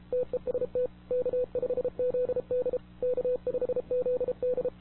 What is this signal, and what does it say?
Morse code, heard as a keyed tone, about 37 wpm, 500 Hz. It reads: TEST K5ZD K5ZD
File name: wwcw08 / k5zd wwcw08 1700z